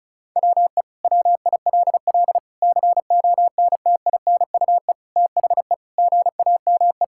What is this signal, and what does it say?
Morse code, heard as a keyed tone, about 36 wpm, 700 Hz. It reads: WE WILL CONTINUE THE GAME
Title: Sentences from Top 1000 Words - Review Part 6 - 35wpm